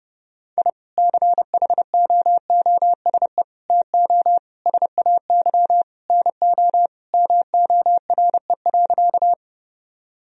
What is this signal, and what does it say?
Morse code, heard as a keyed tone, about 32 wpm, 700 Hz. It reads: I CHOOSE TO SAY NO MORE.
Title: Sentences from Top 1000 Words - Review Part 4 - 30wpm